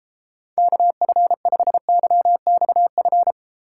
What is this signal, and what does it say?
Morse code, heard as a keyed tone, about 34 wpm, 700 Hz. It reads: KF5YXF